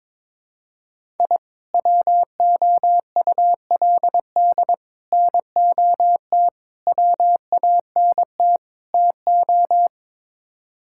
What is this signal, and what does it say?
Morse code, heard as a keyed tone, about 22 wpm, 700 Hz. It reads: I WOULD NOT WANT TO
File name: practice / 22wpm / Build A Phrase - 5-Word Phrases - 22wpm